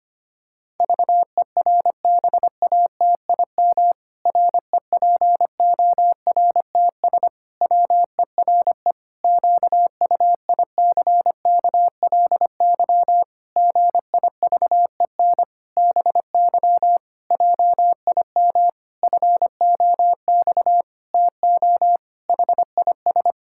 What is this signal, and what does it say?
Morse code, heard as a keyed tone, about 25 wpm, 700 Hz. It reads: VERBATIM REPORTS WERE QUICKLY GIVEN BY JIM FOX TO HIS
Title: Pangram Challenge v2 - 25wpm